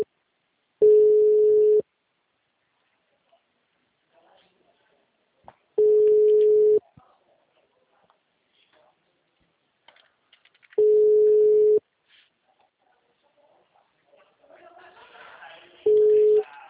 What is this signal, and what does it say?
Ringback tone (Europe, 425 Hz).
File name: out- -2004- - - .15422